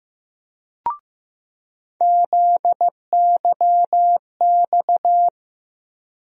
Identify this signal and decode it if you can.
Morse code, heard as a keyed tone, about 15 wpm, 700 Hz. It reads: ZYX
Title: USA Call Sign Postfixes - 3 Letters 15wpm